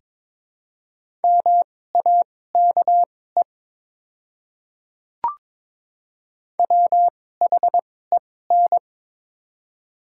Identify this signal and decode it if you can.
Morse code, heard as a keyed tone, about 22 wpm, spaced as for 16 wpm, 690 Hz. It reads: MAKE WHEN
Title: 2x Character Spacing - Top 100 Words - 22wpm